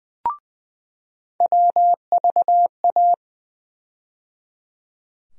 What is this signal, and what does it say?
Morse code, heard as a keyed tone, about 20 wpm, 700 Hz. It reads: WVA
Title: USA Call Sign Postfixes - 3 Letters 20wpm